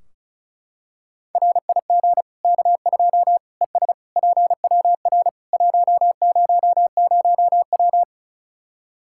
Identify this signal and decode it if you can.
Morse code, heard as a keyed tone, about 34 wpm, 700 Hz. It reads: RIG K2 ES PWR 100W
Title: K1USN SST QSO Practice 35wpm